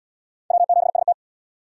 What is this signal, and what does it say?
Morse code, heard as a keyed tone, about 77 wpm, 700 Hz. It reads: W7ST